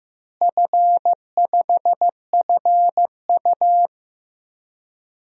Morse code, heard as a keyed tone, about 15 wpm, 700 Hz. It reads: F5FU
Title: DXCC-100 Call Signs - 4 Characters - Repeated 3x - 15wpm